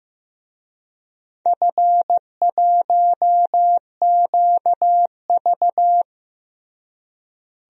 Morse code, heard as a keyed tone, about 15 wpm, 700 Hz. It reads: F1QV
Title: DXCC-100 Call Signs - 4 Characters - Repeated 3x - 15wpm